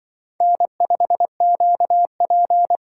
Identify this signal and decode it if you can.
Morse code, heard as a keyed tone, about 24 wpm, 700 Hz. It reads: N5QP
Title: Speed Racing - USA 1x2 Call Signs 20wpm